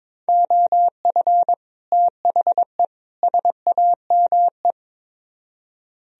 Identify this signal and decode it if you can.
Morse code, heard as a keyed tone, about 24 wpm, 700 Hz. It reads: OF THE SAME
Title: Speed-Racing - Sets of 3 Words 22wpm